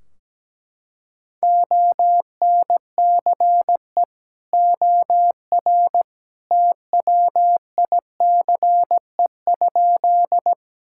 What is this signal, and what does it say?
Morse code, heard as a keyed tone, about 17 wpm, 700 Hz. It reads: ONCE OR TWICE?
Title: Sentences from Top 1600 Words 17wpm